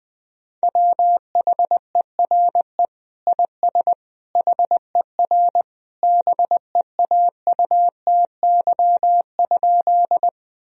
Morse code, heard as a keyed tone, about 20 wpm, 700 Hz. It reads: WHERE IS HER BEAUTY?
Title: Sentences from Top 1000 Words - Review Part 2 - 20wpm